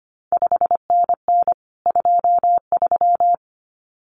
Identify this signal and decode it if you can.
Morse code, heard as a keyed tone, about 25 wpm, 700 Hz. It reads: 5NN 23
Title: CQ WW DX Contest Exchange Encore - 25wpm